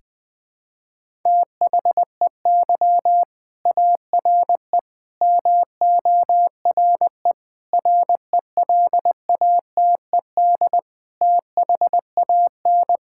Morse code, heard as a keyed tone, about 20 wpm, 700 Hz. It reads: THEY ARE MORE RELATED THAN
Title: Sentences from Top 1000 Words - Review Part 5 - 20wpm